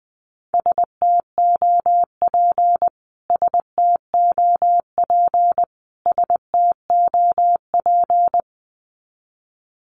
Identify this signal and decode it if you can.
Morse code, heard as a keyed tone, about 20 wpm, 700 Hz. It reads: STOP STOP STOP
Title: Top 500 Words Repeated 20wpm